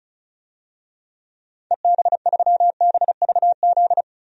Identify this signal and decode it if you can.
Morse code, heard as a keyed tone, about 34 wpm, 700 Hz. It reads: EB3BVZ
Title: DXCC-100 Call Signs - 6 Characters - 35wpm